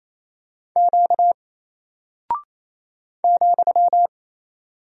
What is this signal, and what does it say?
Morse code, heard as a keyed tone, about 28 wpm, 700 Hz. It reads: Q ,
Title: Single Character (Letter:Number:Punctuation:Prosign) Alt 28wpm